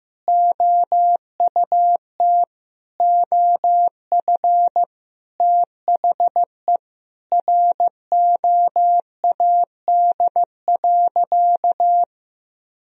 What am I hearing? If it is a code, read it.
Morse code, heard as a keyed tone, about 16 wpm, 700 Hz. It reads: OUT OF THE ROAD.
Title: Sentences from Top 400 Words and Abbreviations 15wpm